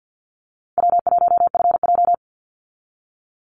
Morse code, heard as a keyed tone, about 49 wpm, 700 Hz. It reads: W1FJ
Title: Easy Call Signs Repeated 2x - 50wpm